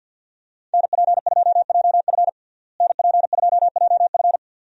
Morse code, heard as a keyed tone, about 51 wpm, 700 Hz. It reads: NP2JF NP2JF
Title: USA Call Signs - 5 Characters - Repeated 2x - 50wpm